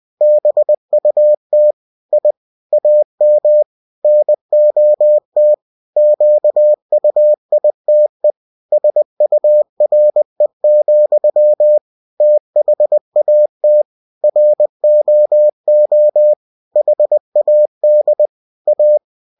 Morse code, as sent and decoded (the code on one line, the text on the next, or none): -... ..- - / .. / .- -- / -. --- - / --.- ..- .. - . / ... ..- .-. --..-- / - .... .- - / .-. --- --- / .... .- -.. / .-
BUT I AM NOT QUITE SURE, THAT ROO HAD A